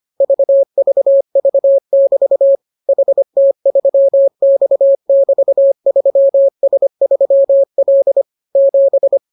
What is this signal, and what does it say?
Morse code, heard as a keyed tone, about 25 wpm, 550 Hz. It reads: VVV<BT> HT3X<BT>3S3L 7